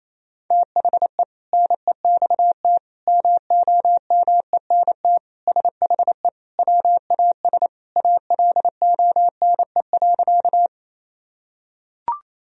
Morse code, heard as a keyed tone, about 28 wpm, 700 Hz. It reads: THE NEXT MOMENT SHE WAS ALONE.
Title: Sentences from Top 1000 Words - Review Part 1 - 28wpm